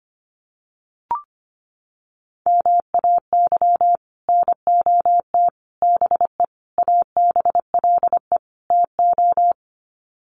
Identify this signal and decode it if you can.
Morse code, heard as a keyed tone, about 25 wpm, 700 Hz. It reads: MAY NOT BE ABLE TO
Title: Sets of 5 Words 25wpm